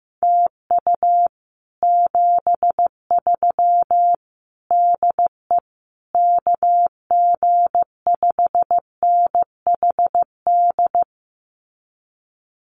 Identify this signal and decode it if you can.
Morse code, heard as a keyed tone, about 15 wpm, 700 Hz. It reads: TU 73 DE KG5NHD
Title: Common QSO Phrases Encore 15wpm